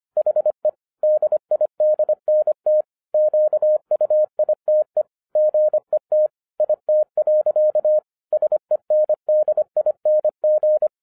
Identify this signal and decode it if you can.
Morse code, heard as a keyed tone, about 25 wpm, 610 Hz. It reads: HE DIDNT QUITE GET IT. SENDING
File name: FIRST FISTS CONTACT-25wpm